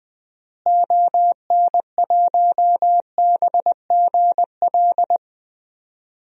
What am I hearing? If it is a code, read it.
Morse code, heard as a keyed tone, about 20 wpm, 700 Hz. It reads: ON1BGL
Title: DXCC-100 Call Signs - 6 Characters - Repeated 2x - 20wpm